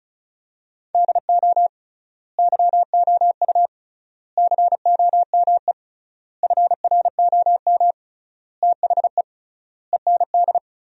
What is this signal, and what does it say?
Morse code, heard as a keyed tone, about 36 wpm, 700 Hz. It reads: DO YOU COME FROM THE END
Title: 3x Word-Spacing - Sentences from Top 200 Words - 35wpm